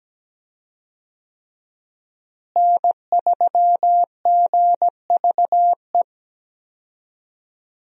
Morse code, heard as a keyed tone, about 17 wpm, 700 Hz. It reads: N3GVE
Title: USA Call Signs - 5 Characters - Repeated 3x - 17wpm